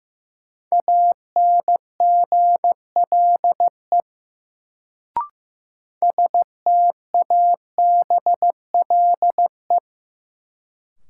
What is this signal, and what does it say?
Morse code, heard as a keyed tone, about 15 wpm, 700 Hz. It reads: ANGLE STABLE
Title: Sentences from Top 1500 Words - New Word Drill - 15wpm